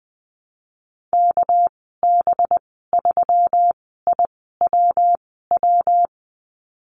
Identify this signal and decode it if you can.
Morse code, heard as a keyed tone, about 20 wpm, spaced as for 14 wpm, 700 Hz. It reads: KB3IWW
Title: 207 Lesson 33g - Letter j - Call Signs 20wpm